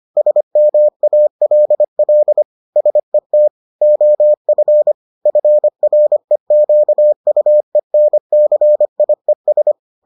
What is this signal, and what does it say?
Morse code, heard as a keyed tone, about 25 wpm, 600 Hz. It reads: SMALL SET OF FREQUENCIES